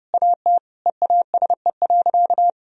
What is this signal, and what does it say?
Morse code, heard as a keyed tone, about 30 wpm, 700 Hz. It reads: AT EASE.
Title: Sentences from Top 1000 Words - Review Part 1 - 30wpm